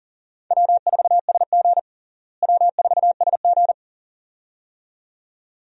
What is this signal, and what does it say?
Morse code, heard as a keyed tone, about 40 wpm, 700 Hz. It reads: W4SG W4SG
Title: USA Call Signs - 4 Characters - Repeated 2x - 40wpm